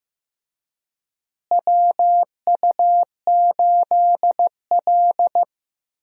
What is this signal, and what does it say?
Morse code, heard as a keyed tone, about 15 wpm, 700 Hz. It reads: WU8L
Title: USA Call Signs - 4 Characters - Repeated 3x - 15wpm